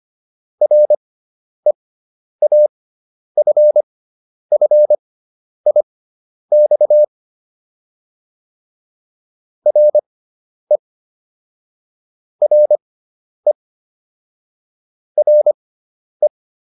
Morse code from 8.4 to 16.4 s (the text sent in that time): RE RE RE